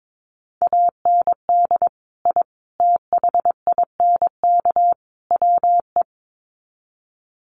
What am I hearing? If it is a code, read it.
Morse code, heard as a keyed tone, about 22 wpm, 700 Hz. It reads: AND I THINK WE
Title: Sets of 4 Words 22wpm